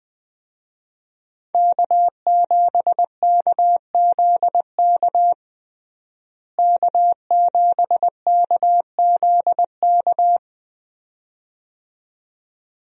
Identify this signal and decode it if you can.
Morse code, heard as a keyed tone, about 21 wpm, 700 Hz. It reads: K7KZK K7KZK
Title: USA Call Signs - 5 Characters - Repeated 2x - 20wpm